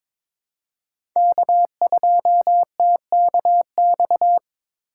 Morse code, heard as a keyed tone, about 22 wpm, 700 Hz. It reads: K2TKX
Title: USA Call Signs - 5 Characters - 6x Character Spacing - 22wpm